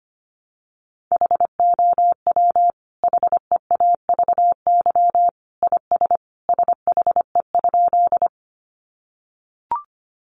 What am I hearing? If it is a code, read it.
Morse code, heard as a keyed tone, about 25 wpm, 700 Hz. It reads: HOW HEAVY IS SHE?
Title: Sentences from Top 1100 Words 25wpm